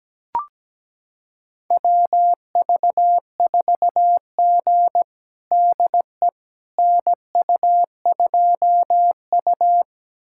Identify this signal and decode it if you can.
Morse code, heard as a keyed tone, about 17 wpm, 700 Hz. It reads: WV4G DE NU2U
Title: USA Callsign DE Callsign - 4 Characters - 17wpm